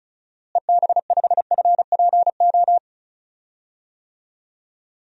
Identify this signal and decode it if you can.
Morse code, heard as a keyed tone, about 34 wpm, 700 Hz. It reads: EB5FPO
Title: DXCC-100 Call Signs - 6 Characters - Repeated 2x - 35wpm